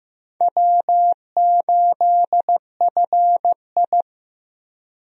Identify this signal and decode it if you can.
Morse code, heard as a keyed tone, about 15 wpm, 700 Hz. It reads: W8FI
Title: USA Call Signs - 4 Characters - Repeated 3x - 15wpm